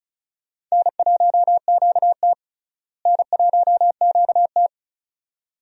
Morse code, heard as a keyed tone, about 36 wpm, 700 Hz. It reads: N1QT N1QT
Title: USA Call Signs - 4 Characters - Repeated 2x - 35wpm